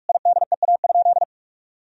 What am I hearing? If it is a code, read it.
Morse code, heard as a keyed tone, about 46 wpm, 700 Hz. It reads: IDEA?
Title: Sentences from Top 1000 Words - Review Part 4 - 45wpm